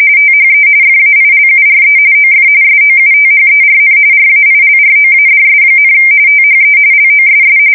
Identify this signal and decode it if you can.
RTTY, about 45.45 baud. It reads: THE FAILURE OF SOUTHERN RAILROADS, THE LOSS
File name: fsk170 / usnavy3